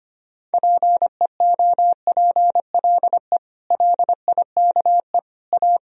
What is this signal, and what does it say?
Morse code, heard as a keyed tone, about 25 wpm, 700 Hz. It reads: PEOPLE LIKE A
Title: Sentences from Top 1000 Words - Review Part 3 - 25wpm